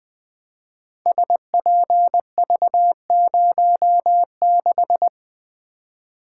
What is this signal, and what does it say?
Morse code, heard as a keyed tone, about 20 wpm, 700 Hz. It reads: SPV06
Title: Code Groups - 5 Letter:Numbers - 20wpm